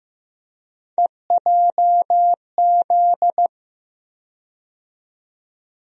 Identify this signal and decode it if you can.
Morse code, heard as a keyed tone, about 15 wpm, 700 Hz. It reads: EJZ
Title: USA Call Sign Postfixes - 3 Letters 15wpm